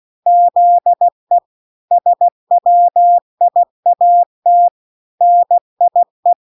Morse code, heard as a keyed tone, about 16 wpm, 700 Hz. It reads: ZE SWIAT NIE